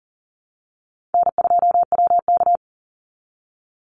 Morse code, heard as a keyed tone, about 39 wpm, 700 Hz. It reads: N2WK
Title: Call Signs Repeated 3x - 40wpm